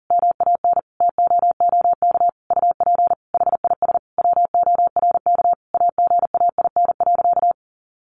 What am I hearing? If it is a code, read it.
Morse code, heard as a keyed tone, about 40 wpm, 700 Hz. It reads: MAN TOOK UP HIS WORK AGAIN.
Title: Sentences from Top 1500 Words 40wpm